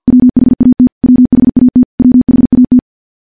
Morse code, heard as a keyed tone, about 50 wpm, 250 Hz. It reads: WHAT WHAT WHAT
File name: morse / morse-training-50WPM